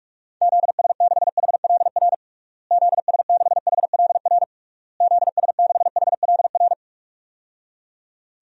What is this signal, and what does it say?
Morse code, heard as a keyed tone, about 41 wpm, 700 Hz. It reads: ZS6HLR ZS6HLR ZS6HLR
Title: DXCC-100 Call Signs - 6 Characters - Repeated 3x - 45wpm